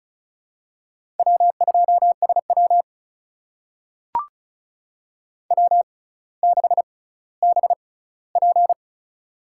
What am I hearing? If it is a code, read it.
Morse code, heard as a keyed tone, about 36 wpm, spaced as for 18 wpm, 700 Hz. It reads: W2SW W6BP
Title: USA Call Signs - 4 Characters - 6x Character Spacing - 35wpm